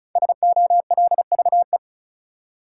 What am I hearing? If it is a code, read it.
Morse code, heard as a keyed tone, about 36 wpm, 700 Hz. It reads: SOLVE
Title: Five Letter Words - 35wpm